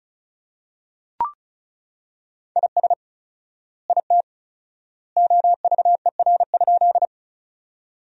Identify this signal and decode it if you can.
Morse code, heard as a keyed tone, about 34 wpm, 700 Hz. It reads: IS IT OVER?
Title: 4x Word-Spacing - Sentences from Top 100 Words - 35wpm